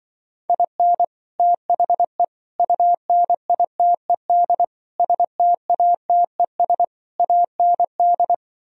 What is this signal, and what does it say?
Morse code, heard as a keyed tone, about 24 wpm, 700 Hz. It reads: IN THE UNITED STATES AND
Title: Speed-Racing - Sets of 5 Words 20wpm